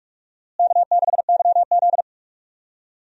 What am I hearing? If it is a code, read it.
Morse code, heard as a keyed tone, about 46 wpm, 700 Hz. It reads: K6YZ
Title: USA Call Signs - 4 Characters - Repeated 3x - 45wpm